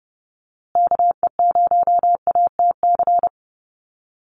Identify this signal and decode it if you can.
Morse code, heard as a keyed tone, about 30 wpm, 700 Hz. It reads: KE0ATC
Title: USA Call Signs Repeated 3x 30wpm